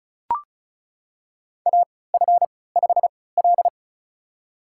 Morse code, heard as a keyed tone, about 36 wpm, spaced as for 20 wpm, 700 Hz. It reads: AF5L
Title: USA Call Signs - 4 Characters - 3x Character Spacing - 35wpm